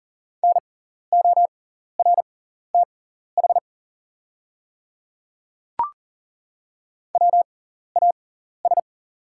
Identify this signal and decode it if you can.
Morse code, heard as a keyed tone, about 40 wpm, spaced as for 14 wpm, 690 Hz. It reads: NORTH WAS
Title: 6x Character Spacing - Top 300 Words - 40wpm